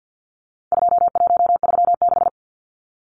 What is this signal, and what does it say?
Morse code, heard as a keyed tone, about 50 wpm, 700 Hz. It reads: 2136
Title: Numbers - 4-Digit 50wpm